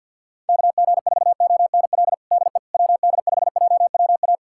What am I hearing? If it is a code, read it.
Morse code, heard as a keyed tone, about 52 wpm, 700 Hz. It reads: KG4ONL DE WD5JWA
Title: USA Callsign DE Callsign - 6 Characters - 50wpm